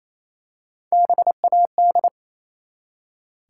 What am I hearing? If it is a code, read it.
Morse code, heard as a keyed tone, about 27 wpm, 700 Hz. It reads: BAD
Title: Three Letter Words - 28wpm